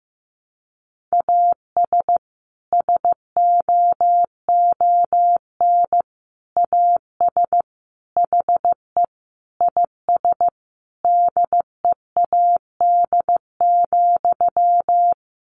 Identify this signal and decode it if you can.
Morse code, heard as a keyed tone, about 15 wpm, 700 Hz. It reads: AS SOON AS HE IS DEAD,
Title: Sentences from Top 1500 Words 15wpm